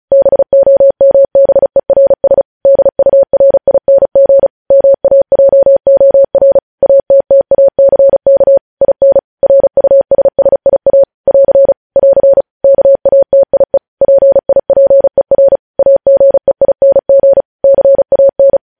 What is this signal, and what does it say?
Morse code, heard as a keyed tone, about 36 wpm, 550 Hz. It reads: BOMBERS DURING MAJOR ATTACK IN RUSSIA <AR> <AR> KATIE PIPER AGEING CAN
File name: podcast / News headlines-1x-35x35